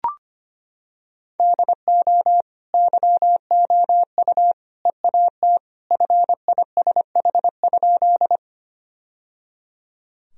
Morse code, heard as a keyed tone, about 25 wpm, 700 Hz. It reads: DO YOU EAT FISH?
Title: Sentences from Top 1000 Words - Review Part 1 - 25wpm